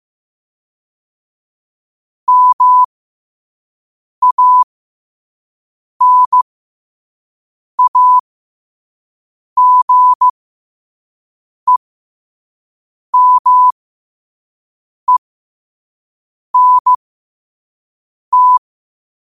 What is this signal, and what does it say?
Morse code, heard as a keyed tone, about 15 wpm, spaced as for 5 wpm, 1000 Hz. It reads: MANAGEMENT